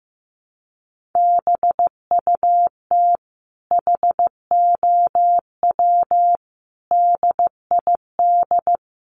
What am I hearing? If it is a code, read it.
Morse code, heard as a keyed tone, about 15 wpm, 700 Hz. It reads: BUT HOW DID
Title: Sentences from Top 900 Words 15wpm